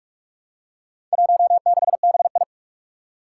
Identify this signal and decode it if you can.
Morse code, heard as a keyed tone, about 44 wpm, 700 Hz. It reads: 16BI
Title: Four Challenging Letter-Number Combinations 45wpm